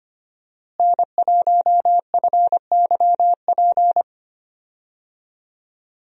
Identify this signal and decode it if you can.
Morse code, heard as a keyed tone, about 25 wpm, 700 Hz. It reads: N1FYP
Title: USA Call Signs - 5 Characters - Repeated 2x - 25wpm